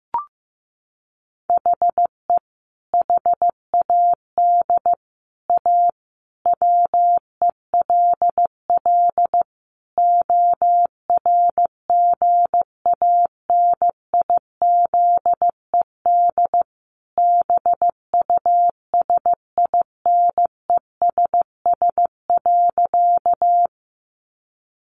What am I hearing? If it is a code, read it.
Morse code, heard as a keyed tone, about 14 wpm, 700 Hz. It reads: HE HAD A WELL ORGANIZED BUSINESS.
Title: Sentences from Top 1300 Words 15wpm